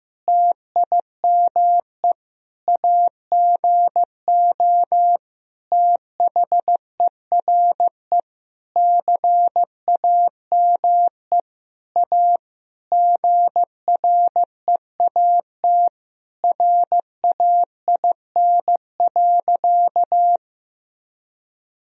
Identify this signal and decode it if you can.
Morse code, heard as a keyed tone, about 15 wpm, 700 Hz. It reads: TIME AGO THERE CAME A GREAT RAIN.
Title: Sentences from Top 1000 Words - Review Part 3 - 15wpm